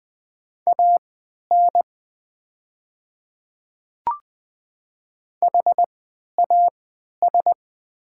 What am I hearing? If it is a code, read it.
Morse code, heard as a keyed tone, about 20 wpm, spaced as for 11 wpm, 700 Hz. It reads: AN HAS
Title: 3x Character Spacing - Top 200 Words - 20wpm